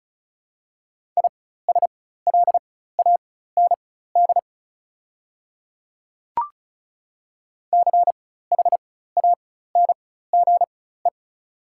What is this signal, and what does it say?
Morse code, heard as a keyed tone, about 36 wpm, spaced as for 16 wpm, 700 Hz. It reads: ISLAND CHANGE
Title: 4x Character Spacing - Top 500 Words - 35wpm